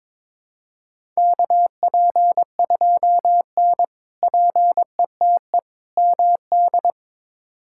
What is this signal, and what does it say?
Morse code, heard as a keyed tone, about 21 wpm, 700 Hz. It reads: KP2N PETE MD
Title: K1USN SST QSO Practice - 22wpm